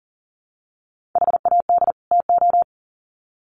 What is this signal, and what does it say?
Morse code, heard as a keyed tone, about 40 wpm, 700 Hz. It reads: HAD TO
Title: Sets of 2 Words 40wpm